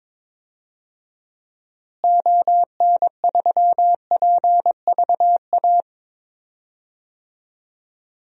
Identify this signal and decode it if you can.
Morse code, heard as a keyed tone, about 22 wpm, 700 Hz. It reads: ON3PVA
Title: DXCC-100 Call Signs - 6 Characters - Repeated 2x - 22wpm